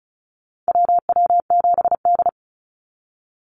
Morse code, heard as a keyed tone, about 34 wpm, 700 Hz. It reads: WW7D
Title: NA SOTA Activator Callsigns 35wpm